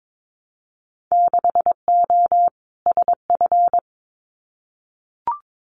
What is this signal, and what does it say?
Morse code, heard as a keyed tone, about 22 wpm, 700 Hz. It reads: 6O SF
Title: Winter Field Day Exchanges Encore 22wpm